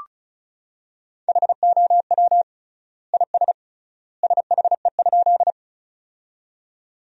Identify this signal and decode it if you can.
Morse code, heard as a keyed tone, about 34 wpm, 700 Hz. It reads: HOW IS SHE?
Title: 3x Word-Spacing - Sentences from Top 100 Words - 35wpm